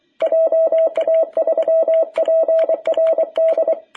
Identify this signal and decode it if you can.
Morse code, heard as a keyed tone, about 25 wpm, 660 Hz, with radio noise. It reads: JA3PRD
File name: jh8cbh / 2021cw